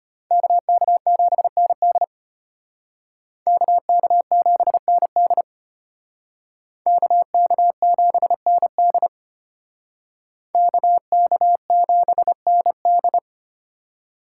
Morse code, heard as a keyed tone, about 32 wpm, 700 Hz. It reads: KK7ND KK7ND KK7ND KK7ND